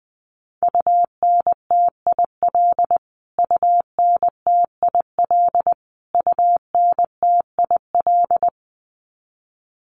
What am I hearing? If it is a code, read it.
Morse code, heard as a keyed tone, about 20 wpm, 700 Hz. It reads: UNTIL UNTIL UNTIL
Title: Top 500 Words Repeated 20wpm